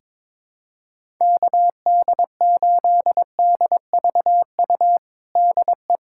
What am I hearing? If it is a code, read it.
Morse code, heard as a keyed tone, about 21 wpm, 700 Hz. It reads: KD8DVU DE
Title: USA Callsign DE Callsign 22wpm